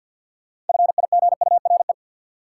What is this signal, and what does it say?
Morse code, heard as a keyed tone, about 50 wpm, 700 Hz. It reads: FIGURE
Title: Sentences from Top 1400 Words - New Word Drill Alt - 50wpm